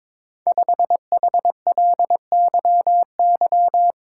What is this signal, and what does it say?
Morse code, heard as a keyed tone, about 22 wpm, 700 Hz. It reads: 5HLYY
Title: Code Groups - 5 Letter:Numbers - 22wpm